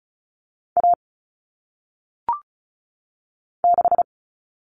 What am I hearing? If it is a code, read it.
Morse code, heard as a keyed tone, about 36 wpm, 700 Hz. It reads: A 6
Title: Common Cut Numbers - 1-Digit 35wpm